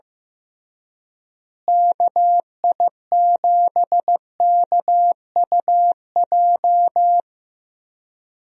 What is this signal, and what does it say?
Morse code, heard as a keyed tone, about 15 wpm, 700 Hz. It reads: KI7KUJ
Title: USA Call Signs - 6 Characters - Repeated 3x - 15wpm